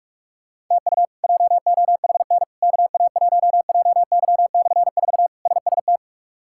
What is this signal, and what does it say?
Morse code, heard as a keyed tone, about 46 wpm, 700 Hz. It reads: TU JOHN KA1JY/4 SST
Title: K1USN SST QSO Practice - 45wpm